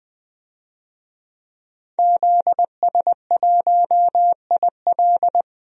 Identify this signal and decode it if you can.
Morse code, heard as a keyed tone, about 20 wpm, 700 Hz. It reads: ZS1IL
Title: DXCC-100 Call Signs - 5 Characters - 20wpm